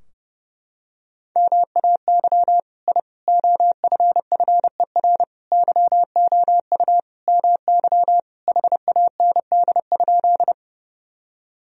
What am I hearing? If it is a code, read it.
Morse code, heard as a keyed tone, about 30 wpm, 700 Hz. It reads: MAY I OFFER YOU MY HAND?
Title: Sentences from Top 1000 Words - Review Part 4 - 30wpm